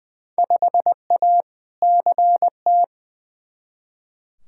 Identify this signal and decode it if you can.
Morse code, heard as a keyed tone, about 20 wpm, 700 Hz. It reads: 5A CT
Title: Field Day Exchanges - 20wpm